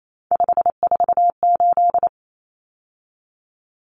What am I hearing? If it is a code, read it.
Morse code, heard as a keyed tone, about 28 wpm, 700 Hz. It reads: 548
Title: Numbers - 3-Digit 28wpm